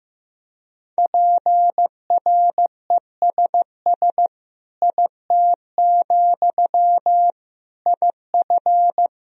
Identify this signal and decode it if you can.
Morse code, heard as a keyed tone, about 15 wpm, 700 Hz. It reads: PRESS IT, IF
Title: Sentences from Top 1000 Words - Review Part 1 - 15wpm